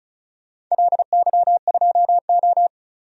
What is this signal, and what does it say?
Morse code, heard as a keyed tone, about 36 wpm, 700 Hz. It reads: LY2O
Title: DXCC-100 Call Signs - 4 Characters - 35wpm